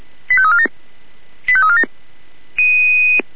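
Five-tone selective call (ZVEI), heard as quick runs of five tones, tones of about 70 ms. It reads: 85347 85347
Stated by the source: ZVEI five-tone selective call received by SDR.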